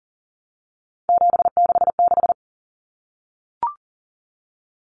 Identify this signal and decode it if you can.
Morse code, heard as a keyed tone, about 40 wpm, 700 Hz. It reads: 766
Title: Numbers - 3-Digit 40wpm